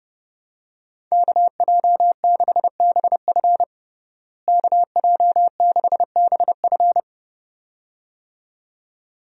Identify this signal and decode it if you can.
Morse code, heard as a keyed tone, about 30 wpm, 700 Hz. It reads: KJ6BF KJ6BF
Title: USA Call Signs - 5 Characters - Repeated 2x - 30wpm